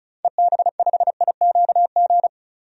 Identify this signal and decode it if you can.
Morse code, heard as a keyed tone, about 36 wpm, 700 Hz. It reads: EB5IQG
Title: DXCC-100 Call Signs - 6 Characters - 35wpm